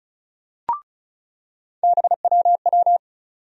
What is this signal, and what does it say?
Morse code, heard as a keyed tone, about 36 wpm, 700 Hz. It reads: BWW
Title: USA Call Sign Postfixes - 3 Letters 35wpm